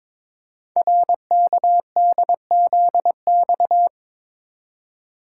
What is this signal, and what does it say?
Morse code, heard as a keyed tone, about 23 wpm, 700 Hz. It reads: RKDZX
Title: Code Groups - 5 Letters - 22wpm